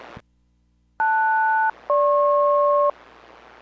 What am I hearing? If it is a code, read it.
Aircraft SELCAL, heard as two pairs of tones. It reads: LS-GP